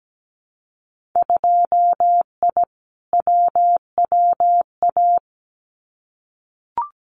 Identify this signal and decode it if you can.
Morse code, heard as a keyed tone, about 17 wpm, 700 Hz. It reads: 2I WWA
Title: Winter Field Day Exchanges Encore 17wpm